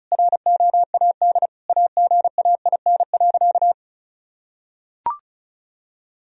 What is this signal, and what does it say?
Morse code, heard as a keyed tone, about 36 wpm, 700 Hz. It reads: ROAD AGAIN.
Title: Sentences from Top 1000 Words - Review Part 2 - 35wpm